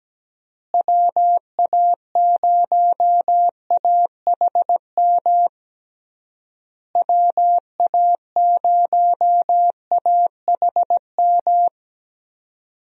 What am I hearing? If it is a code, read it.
Morse code, heard as a keyed tone, about 17 wpm, 700 Hz. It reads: WA0AHM WA0AHM
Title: USA Call Signs - 6 Characters - Repeated 3x - 17wpm